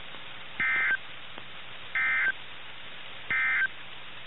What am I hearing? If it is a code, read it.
Emergency Alert System (SAME) tones, heard as harsh buzzing bursts. It is heard from 0.5 to 3.8 s.
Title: NOAA Weather Radio - KIH61